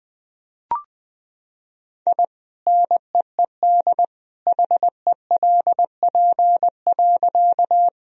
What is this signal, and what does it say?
Morse code, heard as a keyed tone, about 21 wpm, 700 Hz. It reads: I NEED HELP.
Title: Sentences from Top 1000 Words - Review Part 1 - 20wpm